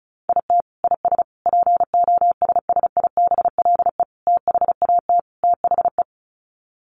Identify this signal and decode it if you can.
Morse code, heard as a keyed tone, about 34 wpm, 700 Hz. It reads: IT IS POSSIBLE THAT THE
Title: Sets of 5 Words Encore2 35wpm